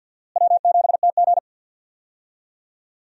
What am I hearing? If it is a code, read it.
Morse code, heard as a keyed tone, about 50 wpm, 700 Hz. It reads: W7TG
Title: USA Call Signs - 4 Characters - Repeated 2x - 50wpm